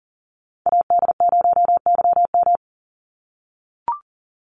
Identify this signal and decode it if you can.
Morse code, heard as a keyed tone, about 40 wpm, 700 Hz. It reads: AD0YM
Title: NA SOTA Chaser Callsigns 40wpm